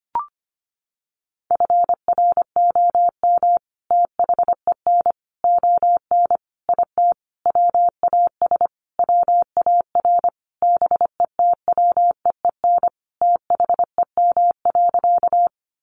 Morse code, heard as a keyed tone, about 25 wpm, 700 Hz. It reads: FROM THEN ON IT WAS WAR BETWEEN THEM.
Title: Sentences from Top 400 Words 25wpm